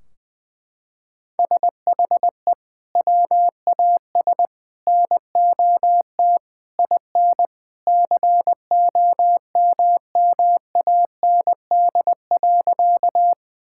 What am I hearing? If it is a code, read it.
Morse code, heard as a keyed tone, about 20 wpm, 700 Hz. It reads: SHE WAS NOT IN COMMAND.
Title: Sentences from Top 1600 Words 20wpm